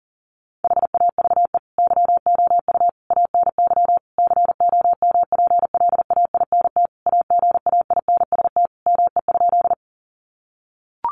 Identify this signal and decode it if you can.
Morse code, heard as a keyed tone, about 40 wpm, 700 Hz. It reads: HAVE YOU ANY COMPLAINT AGAINST ME?
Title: Sentences from Top 1400 Words 40wpm